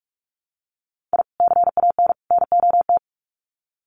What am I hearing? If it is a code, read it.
Morse code, heard as a keyed tone, about 44 wpm, 700 Hz. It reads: I CAN NOT
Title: Sets of 3 Words 45wpm